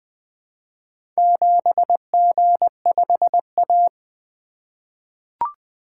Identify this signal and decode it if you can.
Morse code, heard as a keyed tone, about 20 wpm, 700 Hz. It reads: 7G5A
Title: Four Letter-Number Combinations - 20wpm